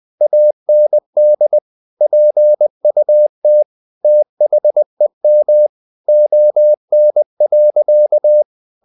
Morse code, heard as a keyed tone, about 20 wpm, 600 Hz. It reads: AND PUT THEM ON.
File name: the old man and the sea / the old man and the sea part 1 20wpm